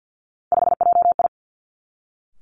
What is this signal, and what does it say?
Morse code, heard as a keyed tone, about 50 wpm, 700 Hz. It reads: 5JI
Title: Three Challenging Letter-Number Combinations 50wpm